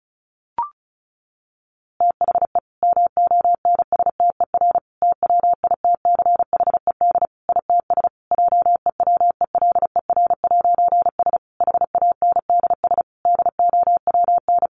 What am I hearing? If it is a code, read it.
Morse code, heard as a keyed tone, about 34 wpm, 700 Hz. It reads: THE MONSTER TWITCHED ITS JEWELER'S HANDS DOWN
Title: A Sound Of Thunder 35wpm